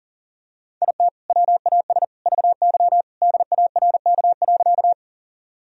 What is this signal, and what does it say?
Morse code, heard as a keyed tone, about 37 wpm, 700 Hz. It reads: IT WAS VY DARK.
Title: Sentences from Top 500 Words and Abbreviations 40wpm